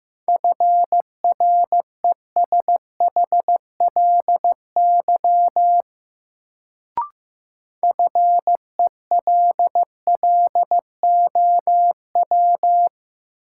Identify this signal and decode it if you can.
Morse code, heard as a keyed tone, about 16 wpm, 700 Hz. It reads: FRESHLY FELLOW
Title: Sets of 5 Words Encore - New Word Drill - 15wpm